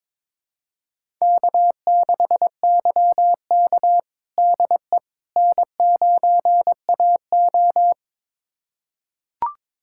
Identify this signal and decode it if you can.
Morse code, heard as a keyed tone, about 22 wpm, 700 Hz. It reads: K6YK DE N9AO
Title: USA Callsign DE Callsign - 4 Characters - 22wpm